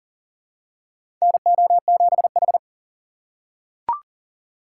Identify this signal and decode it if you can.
Morse code, heard as a keyed tone, about 40 wpm, 700 Hz. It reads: NO7H
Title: USA Call Signs - 4 Characters - 4x Character Spacing - 40wpm